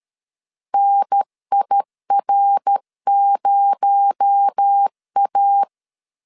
Morse code, heard as a keyed tone, about 13 wpm, 800 Hz. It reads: NIR0A
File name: LEZIONI / 9 28 12wpm 2351